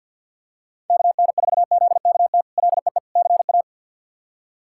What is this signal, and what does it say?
Morse code, heard as a keyed tone, about 50 wpm, 700 Hz. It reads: KN4ZKT LEE CA
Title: K1USN SST QSO Practice - 50wpm